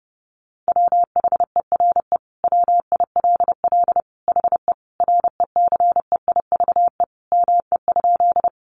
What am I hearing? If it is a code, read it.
Morse code, heard as a keyed tone, about 28 wpm, 700 Hz. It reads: WHERE WILL HE RECEIVE ME?
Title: Sentences from Top 1000 Words 30wpm